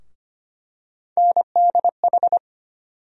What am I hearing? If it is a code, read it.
Morse code, heard as a keyed tone, about 25 wpm, 700 Hz. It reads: NDH
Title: USA Call Sign Postfixes - 3 Letters 25wpm